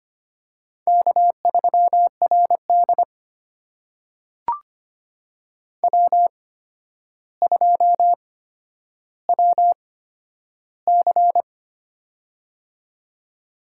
Morse code, heard as a keyed tone, about 25 wpm, spaced as for 11 wpm, 700 Hz. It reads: K3RD W2WC
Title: USA Call Signs - 4 Characters - 8x Character Spacing - 25wpm